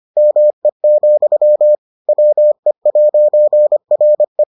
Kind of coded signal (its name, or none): Morse code